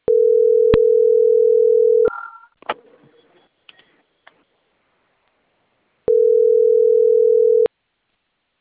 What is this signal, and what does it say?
Ringback tone (North America), 440 + 480 Hz.